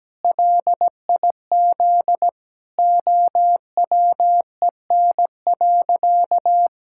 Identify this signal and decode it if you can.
Morse code, heard as a keyed tone, about 17 wpm, 700 Hz. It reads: LIZ OWEN.
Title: Pangram Challenge v2 - 17wpm